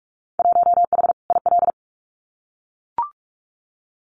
Morse code, heard as a keyed tone, about 44 wpm, 700 Hz. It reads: 1H IL
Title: Winter Field Day Exchanges 45wpm